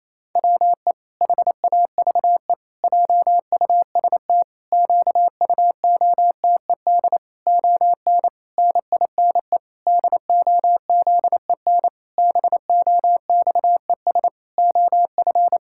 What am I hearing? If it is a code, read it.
Morse code, heard as a keyed tone, about 27 wpm, 700 Hz. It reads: WE HAVE JUST QUOTED ON NINE DOZEN BOXES OF
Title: Pangram Challenge v2 - 28wpm